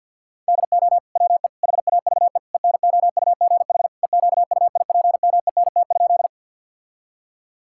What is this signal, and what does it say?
Morse code, heard as a keyed tone, about 51 wpm, 700 Hz. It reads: DO WE HAVE ENOUGH EQUIPMENT?
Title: Sentences from Top 1000 Words - Review Part 5 - 50wpm